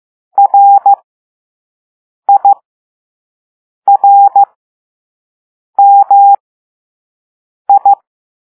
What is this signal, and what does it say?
Morse code, heard as a keyed tone, about 15 wpm, spaced as for 5 wpm, 800 Hz. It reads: RIRMI
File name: Morse / Level2B